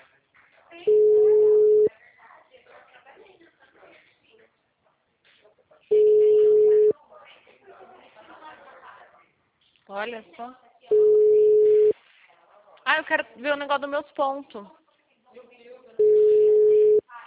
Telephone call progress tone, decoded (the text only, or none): ringback tone